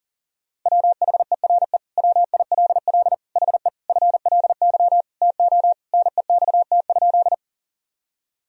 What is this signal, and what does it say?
Morse code, heard as a keyed tone, about 40 wpm, 700 Hz. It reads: WHERE WILL HE FLY TO NEXT?
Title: Sentences from Top 1000 Words - Review Part 2 - 40wpm